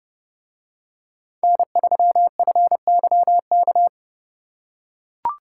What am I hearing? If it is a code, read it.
Morse code, heard as a keyed tone, about 29 wpm, 700 Hz. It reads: N3FYK